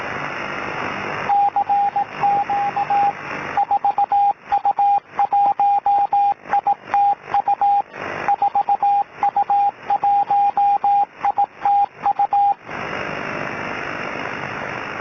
Morse code, heard as a keyed tone, about 18 wpm, 810 Hz, with radio noise. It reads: CQ 4U1ITU 4U1ITU
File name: 4U1ITU